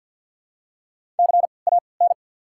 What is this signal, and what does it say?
Morse code, heard as a keyed tone, about 54 wpm, spaced as for 29 wpm, 700 Hz. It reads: CAN
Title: 3x Character Spacing - Top 300 Words - 50wpm